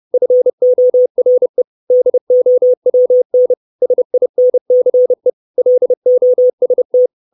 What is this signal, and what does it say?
Morse code, heard as a keyed tone, about 30 wpm, 500 Hz. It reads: FORE DOWN SINCE LOST
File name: 632 Words 30WPM